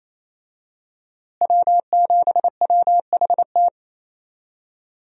Morse code, heard as a keyed tone, about 28 wpm, 700 Hz. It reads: W7WHT
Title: USA Call Signs - 5 Characters - Repeated 3x - 28wpm